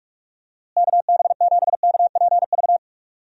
Morse code, heard as a keyed tone, about 46 wpm, 700 Hz. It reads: KB7KPV
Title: USA Call Signs - 6 Characters - Repeated 3x - 45wpm